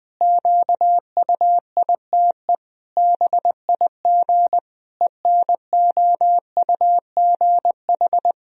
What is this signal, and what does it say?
Morse code, heard as a keyed tone, about 20 wpm, 700 Hz. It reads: QUITE BIG ENOUGH
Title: Sentences from Top 1000 Words - Review Part 2 - 20wpm